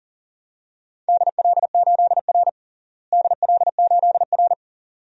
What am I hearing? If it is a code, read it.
Morse code, heard as a keyed tone, about 40 wpm, 700 Hz. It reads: DL8R DL8R
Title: DXCC-100 Call Signs - 4 Characters - Repeated 2x - 40wpm